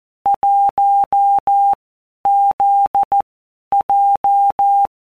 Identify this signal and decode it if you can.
Morse code, heard as a keyed tone, about 14 wpm, 800 Hz. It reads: 1 Z J
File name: Basic Letter Groups By WPM's / Groups@13 / JZ17@13